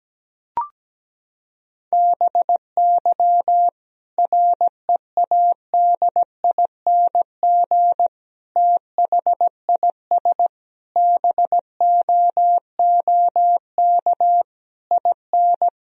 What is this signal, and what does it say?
Morse code, heard as a keyed tone, about 17 wpm, 700 Hz. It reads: BY READING THIS BOOK IN
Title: Sentences from Top 1000 Words - Review Part 3 - 17wpm